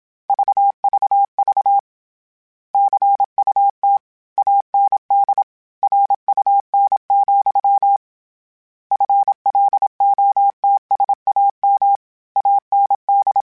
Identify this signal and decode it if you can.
Morse code, heard as a keyed tone, about 27 wpm, 800 Hz. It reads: VVV CUT AND RUN, FLOTSAM AND
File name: Misc / Misc-mp3s / Word Pairs @25